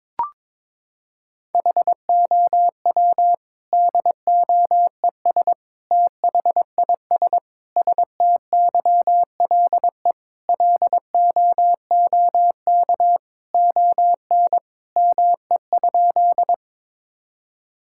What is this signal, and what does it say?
Morse code, heard as a keyed tone, about 22 wpm, 700 Hz. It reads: HOW DOES THIS STYLE LOOK ON ME?
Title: Sentences from Top 1000 Words - Review Part 5 - 22wpm